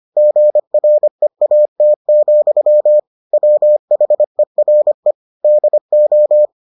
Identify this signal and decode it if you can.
Morse code, heard as a keyed tone, about 25 wpm, 600 Hz. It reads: GREAT, WHERE DO